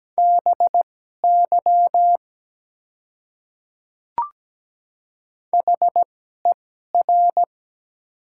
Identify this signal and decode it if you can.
Morse code, heard as a keyed tone, about 17 wpm, spaced as for 12 wpm, 700 Hz. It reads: BY HER
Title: 2x Character Spacing - Top 100 Words - 17wpm